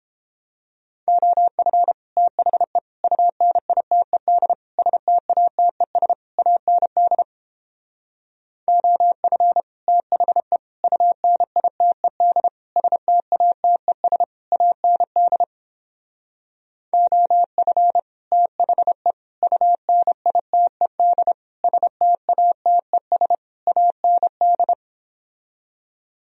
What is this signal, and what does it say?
Morse code, heard as a keyed tone, about 29 wpm, 700 Hz. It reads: OF THE UNITED STATES AND OF THE UNITED STATES AND OF THE UNITED STATES AND